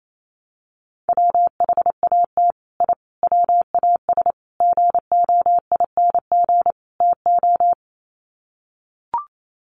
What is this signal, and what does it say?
Morse code, heard as a keyed tone, about 28 wpm, 700 Hz. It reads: WHAT I WAS GOING TO
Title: Sets of 5 Words Encore2 28wpm